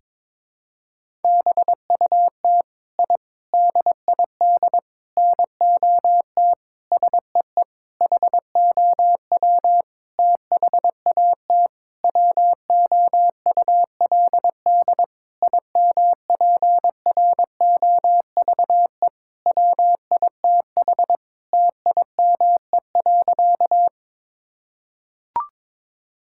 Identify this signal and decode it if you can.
Morse code, heard as a keyed tone, about 21 wpm, 700 Hz. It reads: BUT I DID NOT SEE HOW THAT WOULD IMPROVE WITH TIME.
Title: Sentences from Top 1000 Words - Review Part 5 - 22wpm